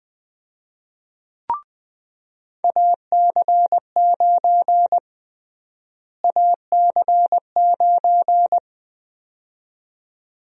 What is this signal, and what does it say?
Morse code, heard as a keyed tone, about 20 wpm, 700 Hz. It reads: AC9 AC9
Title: USA Call Sign Prefixes - Repeated 2x - 20wpm